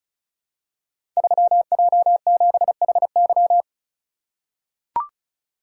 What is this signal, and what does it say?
Morse code, heard as a keyed tone, about 36 wpm, 700 Hz. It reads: 3J7HY
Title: Code Groups - 5 Letter:Numbers - 35wpm